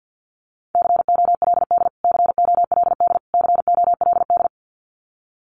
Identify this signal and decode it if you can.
Morse code, heard as a keyed tone, about 50 wpm, 700 Hz. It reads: COLD COLD COLD
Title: Top 500 Words Repeated 50wpm